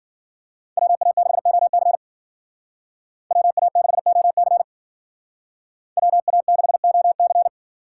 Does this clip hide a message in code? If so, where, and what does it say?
Yes: Morse code from 0.8 to 7.5 s, heard as a keyed tone, about 50 wpm, 700 Hz. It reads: WA6OC WA6OC WA6OC